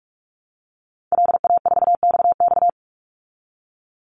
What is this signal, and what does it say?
Morse code, heard as a keyed tone, about 44 wpm, 700 Hz. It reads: LA4XX
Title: Call Signs Repeated 3x - 45wpm